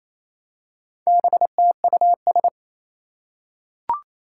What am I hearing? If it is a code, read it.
Morse code, heard as a keyed tone, about 27 wpm, 700 Hz. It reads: BTUS
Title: Four Challenging Letter-Number Combinations 28wpm